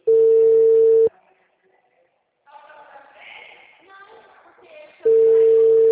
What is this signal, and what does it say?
Ringback tone (China, 450 Hz).
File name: out- -2018- - - .1593